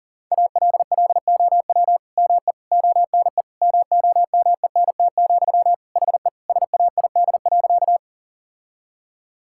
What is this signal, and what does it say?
Morse code, heard as a keyed tone, about 37 wpm, 700 Hz. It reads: ALLOW ME ONE MOMENT, HE SAID.
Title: Sentences from Top 1000 Words - Review Part 5 - 40wpm